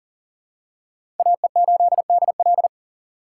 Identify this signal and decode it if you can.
Morse code, heard as a keyed tone, about 40 wpm, 700 Hz. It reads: AE8DL
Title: USA Call Signs - 5 Characters - 3x Character Spacing - 40wpm